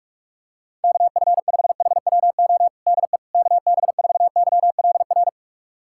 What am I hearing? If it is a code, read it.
Morse code, heard as a keyed tone, about 41 wpm, 700 Hz. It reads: KF5HWO DE KB4YLR
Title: USA Callsign DE Callsign - 6 Characters - 45wpm